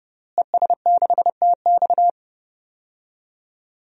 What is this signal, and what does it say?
Morse code, heard as a keyed tone, about 30 wpm, 700 Hz. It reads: ES6TX
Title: DXCC-100 Call Signs - 5 Characters - Repeated 2x - 30wpm